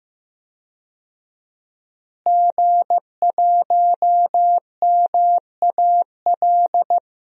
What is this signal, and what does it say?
Morse code, heard as a keyed tone, about 15 wpm, 700 Hz. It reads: G1MAL
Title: DXCC-100 Call Signs - 5 Characters - Repeated 3x - 15wpm